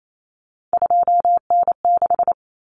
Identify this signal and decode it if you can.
Morse code, heard as a keyed tone, about 28 wpm, 700 Hz. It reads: 2N6
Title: Common Cut Numbers - 3-Digit 28wpm